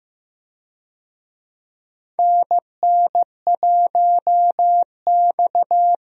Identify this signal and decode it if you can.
Morse code, heard as a keyed tone, about 15 wpm, 700 Hz. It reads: NN1X
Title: USA Call Signs - 4 Characters - Repeated 2x - 15wpm